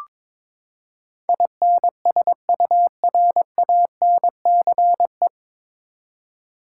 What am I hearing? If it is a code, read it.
Morse code, heard as a keyed tone, about 21 wpm, 700 Hz. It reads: INSURANCE
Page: Morse Code Ninja practice 22wpm